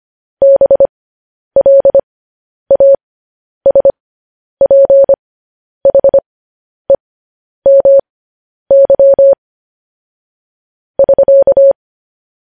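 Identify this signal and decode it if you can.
Morse code, heard as a keyed tone, about 25 wpm, spaced as for 10 wpm, 550 Hz. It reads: BLASPHEMY <SK>
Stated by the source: Morse code news headlines practice, at 10 WPM.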